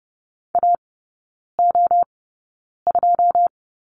Morse code, heard as a keyed tone, about 30 wpm, spaced as for 9 wpm, 700 Hz. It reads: AO2
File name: 081 Lesson 14e - Number 2 - Three Characters 30wpm